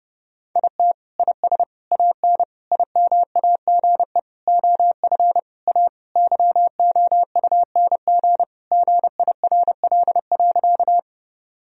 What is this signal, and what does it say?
Morse code, heard as a keyed tone, about 30 wpm, 700 Hz. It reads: IT IS AN IMAGE OF A YOUNG GIRL.
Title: Sentences from Top 1000 Words - Review Part 4 - 30wpm